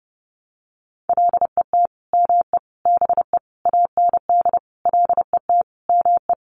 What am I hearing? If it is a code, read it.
Morse code, heard as a keyed tone, about 30 wpm, 700 Hz. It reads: LET ME BE AND LET ME
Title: Sentences from Top 300 Words 30wpm